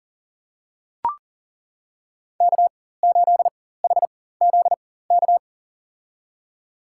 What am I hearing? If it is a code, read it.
Morse code, heard as a keyed tone, about 40 wpm, spaced as for 19 wpm, 700 Hz. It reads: K8HZK
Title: USA Call Signs - 5 Characters - 4x Character Spacing - 40wpm